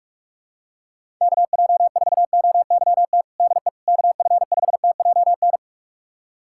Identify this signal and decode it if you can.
Morse code, heard as a keyed tone, about 43 wpm, 700 Hz. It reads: KJ4OYT DE KF5TJN
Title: USA Callsign DE Callsign - 6 Characters - 45wpm